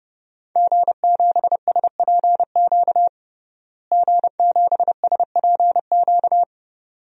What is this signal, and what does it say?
Morse code, heard as a keyed tone, about 30 wpm, 700 Hz. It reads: G7SPQ G7SPQ
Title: DXCC-100 Call Signs - 5 Characters - Repeated 2x - 30wpm